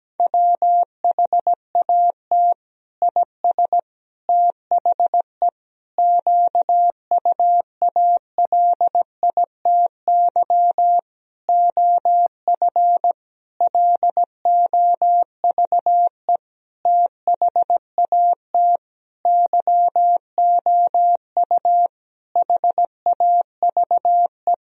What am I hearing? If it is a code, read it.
Morse code, heard as a keyed tone, about 17 wpm, 700 Hz. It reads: WHAT IS THE QUALITY OF LOVE THAT YOU HAVE
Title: Sentences from Top 1000 Words - Review Part 4 - 17wpm